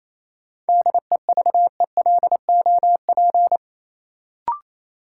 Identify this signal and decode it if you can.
Morse code, heard as a keyed tone, about 28 wpm, 700 Hz. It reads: DEVELOP
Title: Sentences from Top 1000 Words - Review Part 2 - New Word Drill - 28wpm